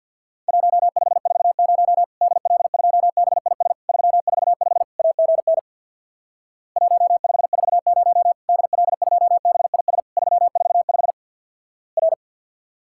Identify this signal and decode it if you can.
Morse code, heard as a keyed tone, about 50 wpm, 700 Hz. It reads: 1540 DL2BIS 345 AGN 1540 DL2BIS 345 R
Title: Worked All Europe DX Contest - QTC Exchange with Repeats - 50wpm